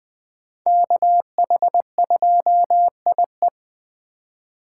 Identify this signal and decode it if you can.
Morse code, heard as a keyed tone, about 20 wpm, 700 Hz. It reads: KH2IE